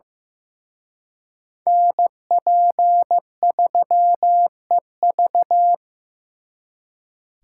Morse code, heard as a keyed tone, about 18 wpm, 700 Hz. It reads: NP3EV
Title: Speed Racing - USA 2x2 Call Signs 15wpm